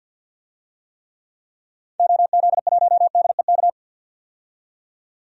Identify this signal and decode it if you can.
Morse code, heard as a keyed tone, about 46 wpm, 700 Hz. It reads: OZ1DEK
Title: DXCC-100 Call Signs - 6 Characters - 50wpm